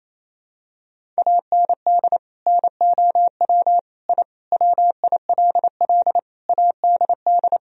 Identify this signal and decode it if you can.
Morse code, heard as a keyed tone, about 28 wpm, 700 Hz. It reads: AND NOW I WILL ADD
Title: Sentences from Top 1000 Words - Review Part 1 - 28wpm